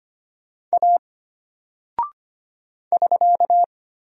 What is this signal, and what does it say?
Morse code, heard as a keyed tone, about 25 wpm, 700 Hz. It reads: A <SK>
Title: Single Character (Letter:Number:Punctuation:Prosign) Alt 25wpm